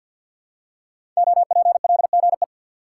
Morse code, heard as a keyed tone, about 46 wpm, 700 Hz. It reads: OPLGE
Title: Code Groups - 5 Letters - 50wpm